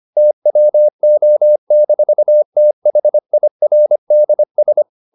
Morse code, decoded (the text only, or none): TWO-THIRDS